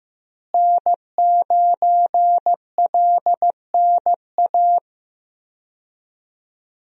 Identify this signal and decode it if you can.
Morse code, heard as a keyed tone, about 15 wpm, 700 Hz. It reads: N9LNA
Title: USA Call Signs - 5 Characters - Repeated 2x - 15wpm